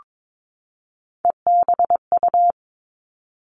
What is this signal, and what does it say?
Morse code, heard as a keyed tone, about 22 wpm, 700 Hz. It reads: EBU
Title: Three Challenging Letter-Number Combinations 22wpm